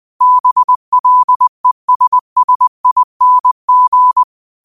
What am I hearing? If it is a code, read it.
Morse code, heard as a keyed tone, about 20 wpm, 1000 Hz. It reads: BLESSING
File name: qotdfiles / 20wpm